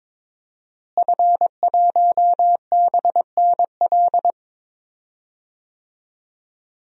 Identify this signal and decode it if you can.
Morse code, heard as a keyed tone, about 22 wpm, 700 Hz. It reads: F1BNL
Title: DXCC-100 Call Signs - 5 Characters - Repeated 2x - 22wpm